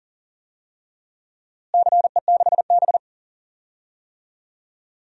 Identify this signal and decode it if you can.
Morse code, heard as a keyed tone, about 40 wpm, 700 Hz. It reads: CE6B
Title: DXCC-100 Call Signs - 4 Characters - Repeated 2x - 40wpm